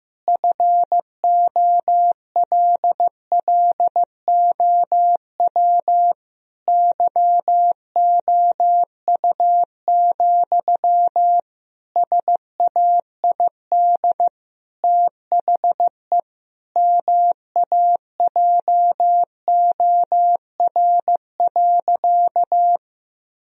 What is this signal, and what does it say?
Morse code, heard as a keyed tone, about 15 wpm, 700 Hz. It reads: FOLLOW YOU, SAID THE MAJOR.
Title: Sentences from Top 1000 Words - Review Part 4 - 15wpm